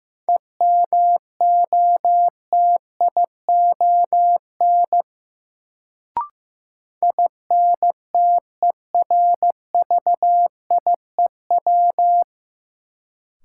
Morse code, heard as a keyed tone, about 15 wpm, 700 Hz. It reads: EMOTION INTERVIEW
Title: Sentences from Top 1600 Words - New Word Drill Alt - 15wpm